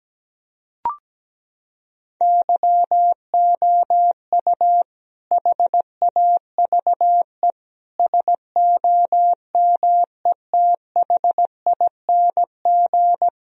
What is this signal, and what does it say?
Morse code, heard as a keyed tone, about 17 wpm, 700 Hz. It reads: YOU HAVE SOMETHING
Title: Sentences from Top 1000 Words - Review Part 4 - 17wpm